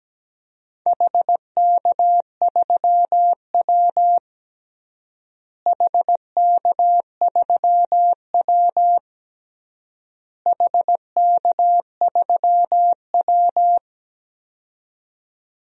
Morse code, heard as a keyed tone, about 17 wpm, 700 Hz. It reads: HK3W HK3W HK3W
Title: DXCC-100 Call Signs - 4 Characters - Repeated 3x - 17wpm